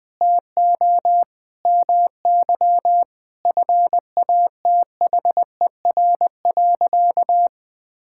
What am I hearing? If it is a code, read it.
Morse code, heard as a keyed tone, about 20 wpm, 700 Hz. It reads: TO MY FATHER.
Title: Sentences from Top 1000 Words - Review Part 5 - 20wpm